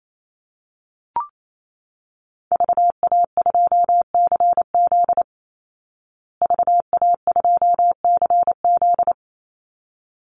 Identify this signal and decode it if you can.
Morse code, heard as a keyed tone, about 28 wpm, 700 Hz. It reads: VA2CZ VA2CZ
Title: Call Signs Repeated 3x - 28wpm